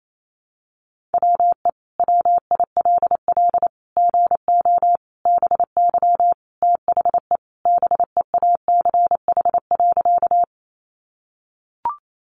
Morse code, heard as a keyed tone, about 28 wpm, 700 Hz. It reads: WE WILL GO BY THE BEACH.
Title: Sentences from Top 1500 Words 28wpm